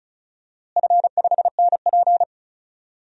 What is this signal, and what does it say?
Morse code, heard as a keyed tone, about 36 wpm, 700 Hz. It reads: F5NP
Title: DXCC-100 Call Signs - 4 Characters - Repeated 3x - 35wpm